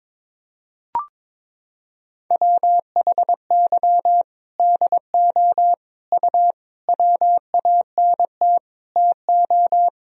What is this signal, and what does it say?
Morse code, heard as a keyed tone, about 22 wpm, 700 Hz. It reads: WHY DO U WANT TO
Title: Sentences from Top 200 Words and Abbreviations 22wpm